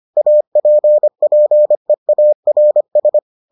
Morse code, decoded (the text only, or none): APPEARS